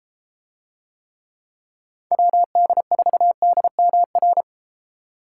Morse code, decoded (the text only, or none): WD4DMR